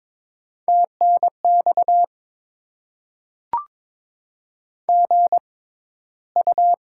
Morse code, heard as a keyed tone, about 22 wpm, spaced as for 13 wpm, 700 Hz. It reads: TNX GU
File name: Top 20 QSO Elements - 6x Character Spacing - 22wpm